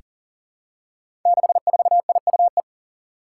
Morse code, heard as a keyed tone, about 39 wpm, 700 Hz. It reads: 64IUE
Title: Five Challenging Letter-Number Combinations 40wpm